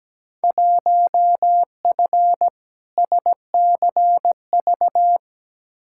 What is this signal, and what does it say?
Morse code, heard as a keyed tone, about 17 wpm, 700 Hz. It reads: 1F SCV
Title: Field Day Exchanges Encore - 17wpm